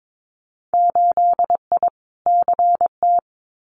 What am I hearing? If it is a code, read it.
Morse code, heard as a keyed tone, about 22 wpm, 700 Hz. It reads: 8I CT
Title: Winter Field Day Exchanges Encore 22wpm